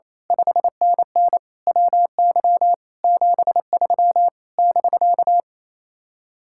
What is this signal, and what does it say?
Morse code, heard as a keyed tone, about 26 wpm, 700 Hz. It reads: 5NN WY 73 <BK>
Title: Speed-Racing Common QSO Phrases Encore - 28wpm